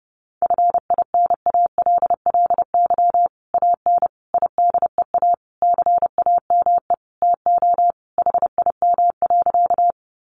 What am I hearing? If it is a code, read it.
Morse code, heard as a keyed tone, about 30 wpm, 700 Hz. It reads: FINALLY AN IDEA CAME TO HIM.
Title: Sentences from Top 1000 Words 30wpm